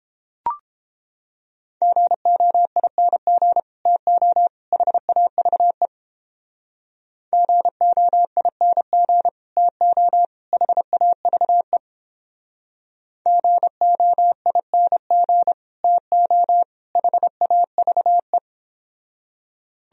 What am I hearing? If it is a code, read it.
Morse code, heard as a keyed tone, about 29 wpm, 700 Hz. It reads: GOING TO HAVE GOING TO HAVE GOING TO HAVE